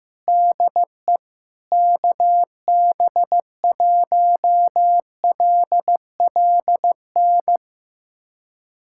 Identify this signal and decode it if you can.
Morse code, heard as a keyed tone, about 16 wpm, 700 Hz. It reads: DE KB1LLN
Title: USA Callsign DE Callsign - 6 Characters - 15wpm